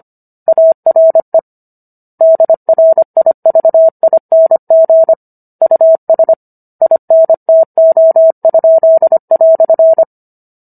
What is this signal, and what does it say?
Morse code, heard as a keyed tone, about 25 wpm, 650 Hz. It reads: ARE DRIVING US INTO?"